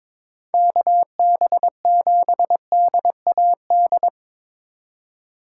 Morse code, heard as a keyed tone, about 25 wpm, 700 Hz. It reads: KB7DAD
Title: Speed Racing - USA 2x3 Call Signs 22wpm